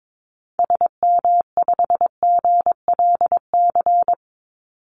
Morse code, heard as a keyed tone, about 22 wpm, 700 Hz. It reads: SM5GLC
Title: Call Signs - 22wpm